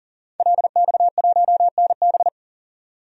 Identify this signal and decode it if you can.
Morse code, heard as a keyed tone, about 38 wpm, 700 Hz. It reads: LX1NB
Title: DXCC-100 Call Signs - 5 Characters - 40wpm